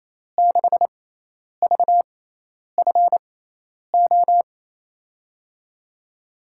Morse code, heard as a keyed tone, about 28 wpm, spaced as for 10 wpm, 700 Hz. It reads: 6VFO
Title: USA Call Signs - 6x Character Spacing - 28wpm